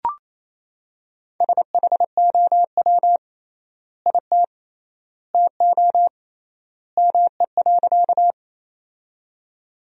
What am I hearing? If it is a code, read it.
Morse code, heard as a keyed tone, about 28 wpm, 700 Hz. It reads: SHOW IT TO ME.
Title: 3x Word-Spacing - Sentences from Top 200 Words - 28wpm